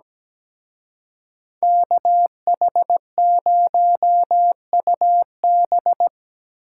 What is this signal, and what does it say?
Morse code, heard as a keyed tone, about 20 wpm, 700 Hz. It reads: KH0UB
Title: Speed Racing - USA 2x2 Call Signs 17wpm